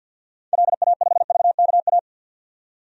Morse code, heard as a keyed tone, about 50 wpm, 700 Hz. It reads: LA5VKA
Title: DXCC-100 Call Signs - 6 Characters - 50wpm